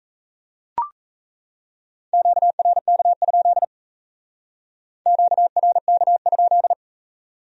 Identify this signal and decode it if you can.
Morse code, heard as a keyed tone, about 36 wpm, 700 Hz. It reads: QRK? QRK?